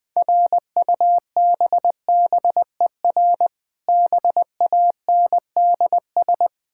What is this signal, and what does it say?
Morse code, heard as a keyed tone, about 20 wpm, 700 Hz. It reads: RUBBER BANDS
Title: Difficult Words with Hints - 20wpm